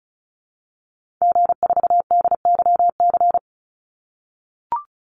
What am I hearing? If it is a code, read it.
Morse code, heard as a keyed tone, about 34 wpm, 700 Hz. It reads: G4DYC
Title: Call Signs - 35wpm